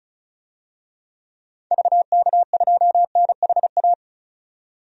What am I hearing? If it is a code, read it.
Morse code, heard as a keyed tone, about 36 wpm, 700 Hz. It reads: VK2NHA
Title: DXCC-100 Call Signs - 6 Characters - 35wpm